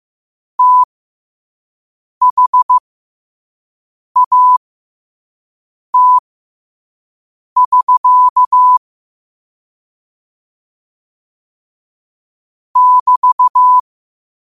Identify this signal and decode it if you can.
Morse code, heard as a keyed tone, about 15 wpm, spaced as for 5 wpm, 1000 Hz. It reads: THAT<SK> <BT>
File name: qotdfiles / 5wpm